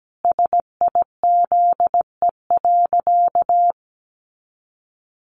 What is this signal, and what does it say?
Morse code, heard as a keyed tone, about 17 wpm, 700 Hz. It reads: SIZE.
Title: Sentences from Top 400 Words 17wpm